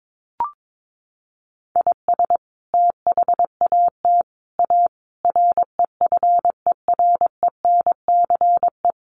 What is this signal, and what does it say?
Morse code, heard as a keyed tone, about 22 wpm, 700 Hz. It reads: IS THAT A REFERENCE
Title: Sentences from Top 1200 Words 22wpm